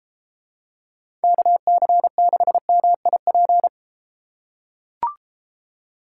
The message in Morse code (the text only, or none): KC6MIP